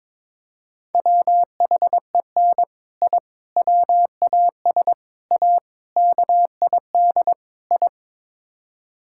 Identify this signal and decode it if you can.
Morse code, heard as a keyed tone, about 22 wpm, 700 Hz. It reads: WHEN I WAS A KID I
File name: practice / 22wpm / Sets of 6 Words - 22wpm